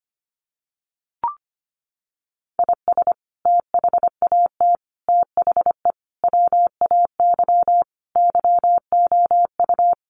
Morse code, heard as a keyed tone, about 25 wpm, 700 Hz. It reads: IS THAT THE WAY YOU
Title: Sentences from Top 1300 Words 25wpm